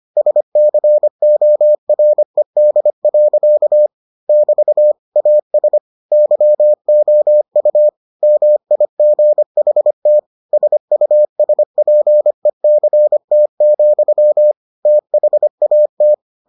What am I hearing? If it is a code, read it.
Morse code, heard as a keyed tone, about 25 wpm, 600 Hz. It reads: SCORED. <BT> AS YOU MIGHT SUSPECT, THAT